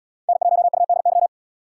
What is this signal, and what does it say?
Morse code, heard as a keyed tone, about 77 wpm, 700 Hz. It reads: N1UDY